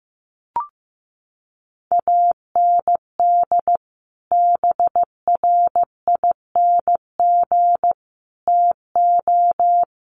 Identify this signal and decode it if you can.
Morse code, heard as a keyed tone, about 15 wpm, 700 Hz. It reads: AND BRING TO
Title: Sets of 5 Words Encore 15wpm